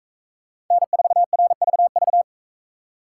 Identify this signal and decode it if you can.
Morse code, heard as a keyed tone, about 41 wpm, 700 Hz. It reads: N4RVV